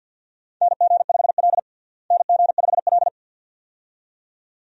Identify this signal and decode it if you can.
Morse code, heard as a keyed tone, about 49 wpm, 700 Hz. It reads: NG5L NG5L
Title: USA Call Signs - 4 Characters - Repeated 2x - 50wpm